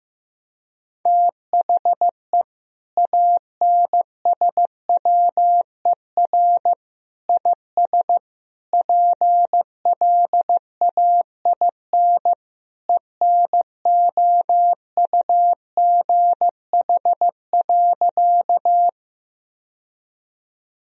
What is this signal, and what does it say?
Morse code, heard as a keyed tone, about 16 wpm, 700 Hz. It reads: THE ANSWER IS PLAIN ENOUGH.
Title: Sentences from Top 1000 Words - Review Part 2 - 15wpm